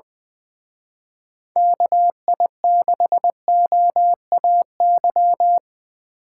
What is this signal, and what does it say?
Morse code, heard as a keyed tone, about 24 wpm, 700 Hz. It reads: KI6OAY
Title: Speed Racing - USA 2x3 Call Signs 20wpm